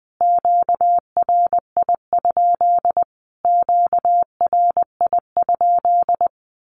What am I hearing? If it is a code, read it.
Morse code, heard as a keyed tone, about 20 wpm, 700 Hz. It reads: QRI? QRI?
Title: Q-Codes Repeated 3x 20wpm